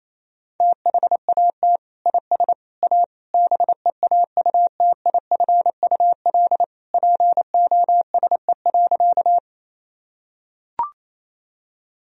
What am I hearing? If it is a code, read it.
Morse code, heard as a keyed tone, about 28 wpm, 700 Hz. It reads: THAT IS A BEAUTIFUL POSE.
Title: Sentences from Top 1000 Words - Review Part 2 - 28wpm